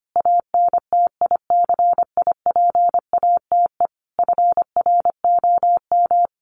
Morse code, heard as a keyed tone, about 25 wpm, 700 Hz. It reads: ANTICIPATE FROM
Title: Sentences from Top 1500 Words 25wpm